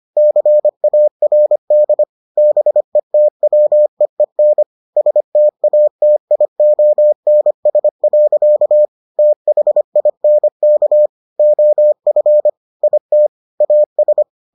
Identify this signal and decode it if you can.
Morse code, heard as a keyed tone, about 25 wpm, 600 Hz. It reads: CARD BETWEEN STATIONS. THINK OF IT AS